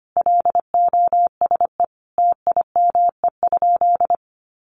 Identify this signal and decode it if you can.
Morse code, heard as a keyed tone, about 25 wpm, 700 Hz. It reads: LOSE TIME?
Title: Sentences from Top 1100 Words 25wpm